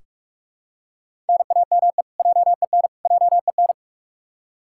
Morse code, heard as a keyed tone, about 44 wpm, 700 Hz. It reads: NAME JEN JEN
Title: K1USN SST QSO Practice 45wpm